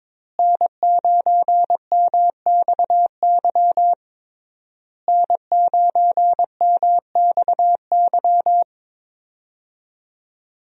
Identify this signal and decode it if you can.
Morse code, heard as a keyed tone, about 22 wpm, 700 Hz. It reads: N9MXY N9MXY
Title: USA Call Signs - 5 Characters - Repeated 3x - 22wpm